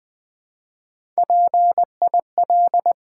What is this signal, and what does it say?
Morse code, heard as a keyed tone, about 20 wpm, 700 Hz. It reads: PIL
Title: Build A Word - 1 Letter At A Time - 4-Letter Words - 20wpm